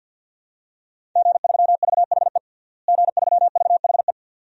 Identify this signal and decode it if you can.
Morse code, heard as a keyed tone, about 46 wpm, 700 Hz. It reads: G3VHE G3VHE
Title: DXCC-100 Call Signs - 5 Characters - Repeated 2x - 50wpm